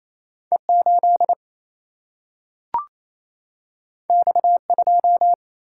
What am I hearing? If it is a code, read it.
Morse code, heard as a keyed tone, about 28 wpm, 700 Hz. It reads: E8 X2
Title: Single-Letter Single-Number 28wpm